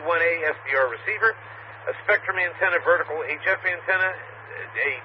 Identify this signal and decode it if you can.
Emergency Alert System (SAME) tones, heard as harsh buzzing bursts, with speech. It is heard from 1.0 to 4.2 s.